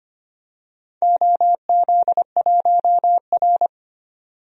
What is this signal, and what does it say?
Morse code, heard as a keyed tone, about 25 wpm, 700 Hz. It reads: OZ1R
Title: DXCC-100 Call Signs - 4 Characters - 25wpm